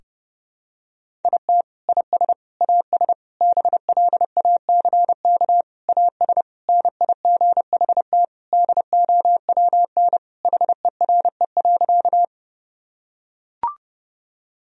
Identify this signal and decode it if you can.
Morse code, heard as a keyed tone, about 30 wpm, 700 Hz. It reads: IT IS AS BLACK AS NIGHT DOWN HERE.
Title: Sentences from Top 1000 Words - Review Part 2 - 30wpm